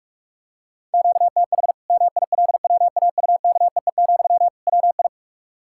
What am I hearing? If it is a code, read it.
Morse code, heard as a keyed tone, about 48 wpm, 700 Hz. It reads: QTH MILWAUKEE, WI
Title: Practice QSO - Exchange 1 - Beginning - 45wpm